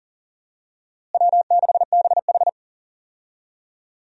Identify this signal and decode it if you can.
Morse code, heard as a keyed tone, about 40 wpm, 700 Hz. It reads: W6BH
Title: USA Call Signs - 4 Characters - Repeated 2x - 40wpm